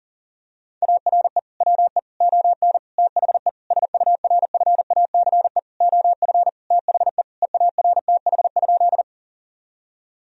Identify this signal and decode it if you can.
Morse code, heard as a keyed tone, about 40 wpm, 700 Hz. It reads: ARE WE ON THE SURFACE OF THE EARTH?
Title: Sentences from Top 1000 Words - Review Part 3 - 40wpm